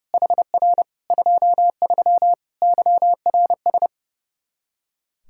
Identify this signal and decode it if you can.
Morse code, heard as a keyed tone, about 32 wpm, 700 Hz. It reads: HR 23 YRS
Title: K1USN SST QSO Practice 30wpm